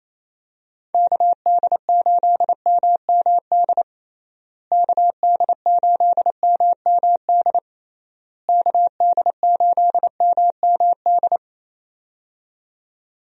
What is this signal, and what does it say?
Morse code, heard as a keyed tone, about 28 wpm, 700 Hz. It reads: KD8MMD KD8MMD KD8MMD
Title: USA Call Signs - 6 Characters - Repeated 3x - 28wpm